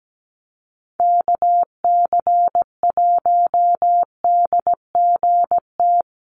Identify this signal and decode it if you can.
Morse code, heard as a keyed tone, about 17 wpm, 700 Hz. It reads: KC1DGT
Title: USA Call Signs Repeated 3x 17wpm